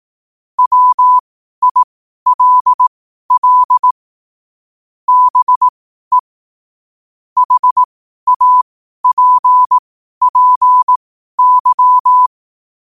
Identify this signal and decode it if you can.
Morse code, heard as a keyed tone, about 18 wpm, spaced as for 12 wpm, 1000 Hz. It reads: WILL BE HAPPY
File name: 12wpm